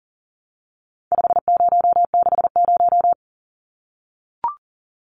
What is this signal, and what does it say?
Morse code, heard as a keyed tone, about 40 wpm, 700 Hz. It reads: 5060
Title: Numbers - 4-Digit 40wpm